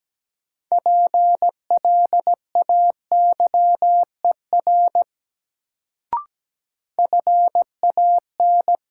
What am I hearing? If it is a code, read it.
Morse code, heard as a keyed tone, about 17 wpm, 700 Hz. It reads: PLAYER FAN
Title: Sentences from Top 1000 Words - Review Part 4 - New Word Drill - 17wpm